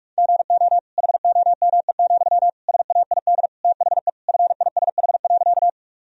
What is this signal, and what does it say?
Morse code, heard as a keyed tone, about 46 wpm, 700 Hz. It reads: GO HOME, SAID THE FISH.
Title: Sentences from Top 1000 Words - Review Part 1 - 45wpm